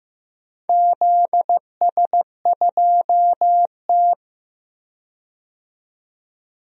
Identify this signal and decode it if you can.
Morse code, heard as a keyed tone, about 15 wpm, 700 Hz. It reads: ZS2T
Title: DXCC-100 Call Signs - 4 Characters - 15wpm